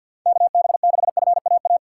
Morse code, heard as a keyed tone, about 53 wpm, 700 Hz. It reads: KB6FAA
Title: USA Call Signs - 6 Characters - 3x Character Spacing - 50wpm